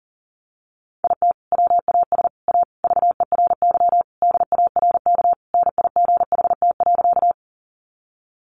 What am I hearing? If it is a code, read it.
Morse code, heard as a keyed tone, about 40 wpm, 700 Hz. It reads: IT WAS A VERY DARK NIGHT.
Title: Sentences from Top 500 Words 40wpm